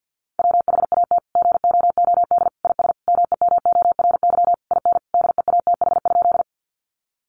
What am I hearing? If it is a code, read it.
Morse code, heard as a keyed tone, about 50 wpm, 700 Hz. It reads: WHAT GOOD IS MEMORY IN DEATH?
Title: Sentences from Top 1300 Words 50wpm